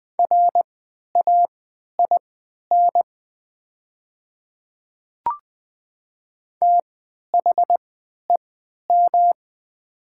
Morse code, heard as a keyed tone, about 20 wpm, spaced as for 11 wpm, 700 Hz. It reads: RAIN THEM
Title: 3x Character Spacing - Top 500 Words - 20wpm